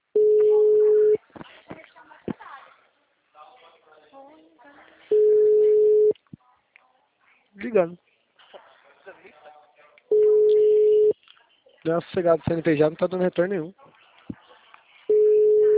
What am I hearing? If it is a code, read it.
Ringback tone (Europe, 425 Hz), with speech.